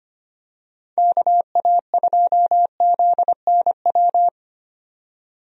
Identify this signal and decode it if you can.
Morse code, heard as a keyed tone, about 25 wpm, 700 Hz. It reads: KA2ZNW
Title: USA Call Signs - 6 Characters - 25wpm